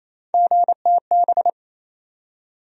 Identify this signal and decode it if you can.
Morse code, heard as a keyed tone, about 28 wpm, 700 Hz. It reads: GTB
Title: USA Call Sign Postfixes - 3 Letters 28wpm